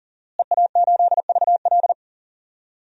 Morse code, heard as a keyed tone, about 39 wpm, 700 Hz. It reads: EA8VL
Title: DXCC-100 Call Signs - 5 Characters - Repeated 3x - 40wpm